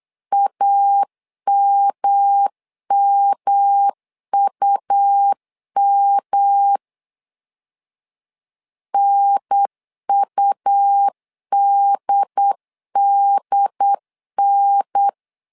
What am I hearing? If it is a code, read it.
Morse code, heard as a keyed tone, about 8 wpm, 800 Hz. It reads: AMMUM NUDDN
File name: LEZIONI / 3 MWRO / 3 M 8wpm 2300